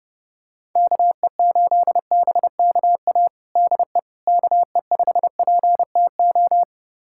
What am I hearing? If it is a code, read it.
Morse code, heard as a keyed tone, about 32 wpm, 700 Hz. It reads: KE8BKA DE KE5PTO
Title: USA Callsign DE Callsign - 6 Characters - 30wpm